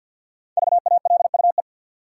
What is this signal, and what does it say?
Morse code, heard as a keyed tone, about 50 wpm, 700 Hz. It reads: VALUE
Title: Sentences from Top 600 Words - New Word Drill - 50wpm